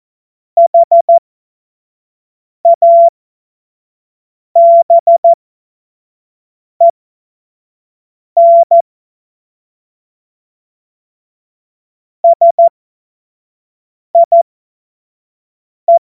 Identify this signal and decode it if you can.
Morse code, heard as a keyed tone, about 14 wpm, spaced as for 5 wpm, 680 Hz. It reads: HABEN SIE
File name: Morsekurs / Witze / Witz 30. Mai 2023 25 F70